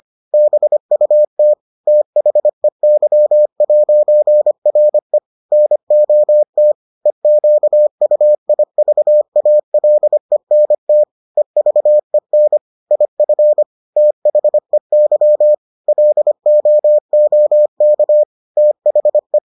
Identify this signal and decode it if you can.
Morse code, heard as a keyed tone, about 25 wpm, 600 Hz. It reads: BUT THEY'RE NOT EQUIVALENT EVEN IF THEY LOOK THE